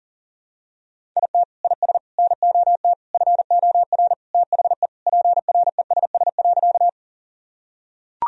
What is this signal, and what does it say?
Morse code, heard as a keyed tone, about 40 wpm, 700 Hz. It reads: IT IS NOT FOR THE PRESS.
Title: Sentences from Top 1000 Words - Review Part 1 - 40wpm